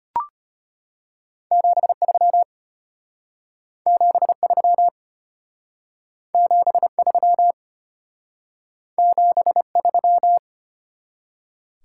Morse code, heard as a keyed tone, about 32 wpm, 700 Hz. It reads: 73 73 73 73